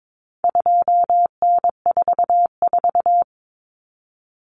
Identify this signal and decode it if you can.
Morse code, heard as a keyed tone, about 22 wpm, 700 Hz. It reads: 2N44
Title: Common Cut Numbers - 4-Digit 22wpm